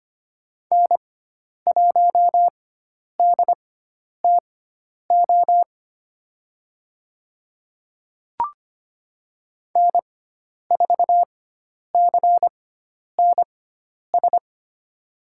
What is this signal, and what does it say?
Morse code, heard as a keyed tone, about 25 wpm, spaced as for 10 wpm, 700 Hz. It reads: N1DTO N4CNS
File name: Lesson 16g - Letter c - Call Signs 25wpm